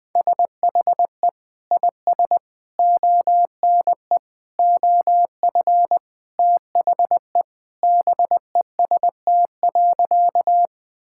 Morse code, heard as a keyed tone, about 20 wpm, 700 Hz. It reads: SHE IS ONE OF THE BEST.
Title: Sentences with 2-Word Phrases - Spoken Beginning - 20wpm